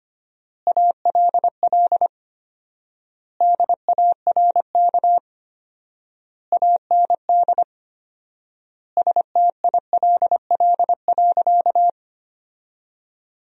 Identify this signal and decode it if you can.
Morse code, heard as a keyed tone, about 25 wpm, 700 Hz. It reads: ALL DARK AND STILL.
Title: 4x Word-Spacing - Sentences from Top 500 Words - 25wpm